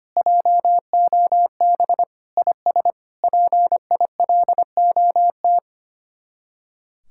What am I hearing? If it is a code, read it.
Morse code, heard as a keyed tone, about 31 wpm, 700 Hz. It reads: JOB IS PILOT
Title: Speed-Racing Common QSO Phrases Encore - 25wpm